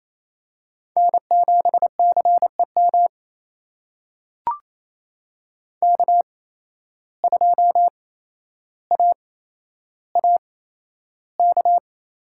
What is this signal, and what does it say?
Morse code, heard as a keyed tone, about 28 wpm, spaced as for 12 wpm, 700 Hz. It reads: N7CEM K2AAK
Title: USA Call Signs - 5 Characters - 8x Character Spacing - 28wpm